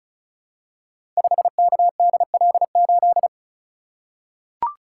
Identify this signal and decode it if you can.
Morse code, heard as a keyed tone, about 34 wpm, 700 Hz. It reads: 5KDL8
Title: Code Groups - 5 Letter:Numbers - 35wpm